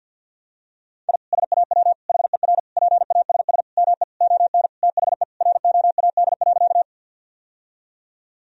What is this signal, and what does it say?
Morse code, heard as a keyed tone, about 50 wpm, 700 Hz. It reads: I SAW HER PASS ME ON THE ROAD.
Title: Sentences from Top 1000 Words - Review Part 2 - 50wpm